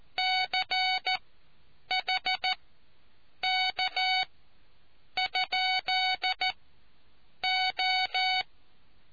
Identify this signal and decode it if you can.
Morse code, heard as a keyed tone, about 14 wpm, spaced as for 7 wpm, 750 Hz. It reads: CHK?O